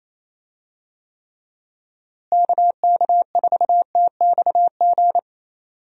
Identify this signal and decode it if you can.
Morse code, heard as a keyed tone, about 28 wpm, 700 Hz. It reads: KK4TXG
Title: USA Call Signs - 6 Characters - Repeated 2x - 28wpm